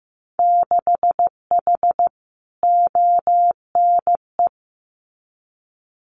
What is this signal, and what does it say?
Morse code, heard as a keyed tone, about 15 wpm, 700 Hz. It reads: 6H ONE
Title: Winter Field Day Exchanges Encore 15wpm